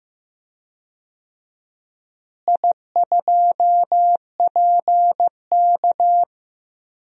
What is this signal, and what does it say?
Morse code, heard as a keyed tone, about 15 wpm, 700 Hz. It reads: I2PK
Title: DXCC-100 Call Signs - 4 Characters - Repeated 3x - 15wpm